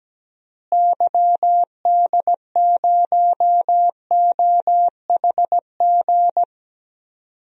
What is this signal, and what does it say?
Morse code, heard as a keyed tone, about 17 wpm, 700 Hz. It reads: YD0OHG
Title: DXCC-100 Call Signs - 6 Characters - Repeated 2x - 17wpm